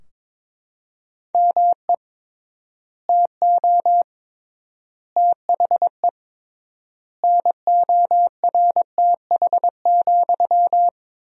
Morse code, heard as a keyed tone, about 22 wpm, 700 Hz. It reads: ME TO THE NORTH,
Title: 3x Word-Spacing - Sentences from Top 400 Words - 22wpm